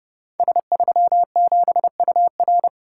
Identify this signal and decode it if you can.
Morse code, heard as a keyed tone, about 31 wpm, 700 Hz. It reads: S37UR
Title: Code Groups - 5 Letter:Numbers - 30wpm